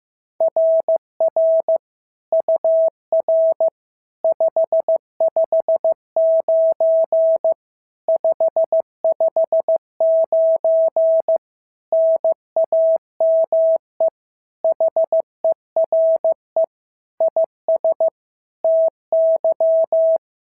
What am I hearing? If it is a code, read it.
Morse code, heard as a keyed tone, about 14 wpm, 650 Hz. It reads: RR UR 559 559 NAME HERE IS TY
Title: SOTA QSO Practice 15wpm